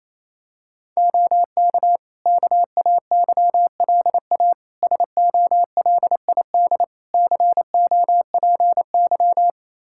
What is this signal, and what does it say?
Morse code, heard as a keyed tone, about 28 wpm, 700 Hz. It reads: OK KAYLA SOLID COPY
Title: K1USN SST QSO Practice 28wpm